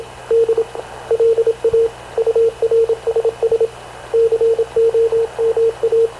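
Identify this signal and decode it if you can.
Morse code, heard as a keyed tone, about 26 wpm, 460 Hz, with radio noise. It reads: D LA URSS COMA
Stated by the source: Havana Radio CLA press (PX) broadcast, recorded by Radio Officer Paul Dumesnil aboard mv CNAV QUEST/CZDO.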